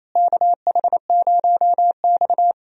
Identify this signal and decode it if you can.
Morse code, heard as a keyed tone, about 28 wpm, 700 Hz. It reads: KH0X
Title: USA Call Signs - 4 Characters - Repeated 2x - 28wpm